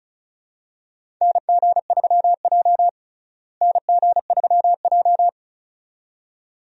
Morse code, heard as a keyed tone, about 36 wpm, 700 Hz. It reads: NG3J NG3J
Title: USA Call Signs - 4 Characters - Repeated 2x - 35wpm